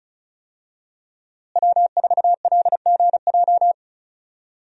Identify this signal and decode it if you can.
Morse code, heard as a keyed tone, about 36 wpm, 700 Hz. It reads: W4LGJ
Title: USA Call Signs - 5 Characters - Repeated 2x - 35wpm